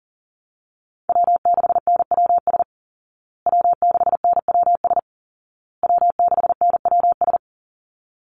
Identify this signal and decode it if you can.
Morse code, heard as a keyed tone, about 37 wpm, 700 Hz. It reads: W6NWS W6NWS W6NWS
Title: Call Signs Repeated 2x - 40wpm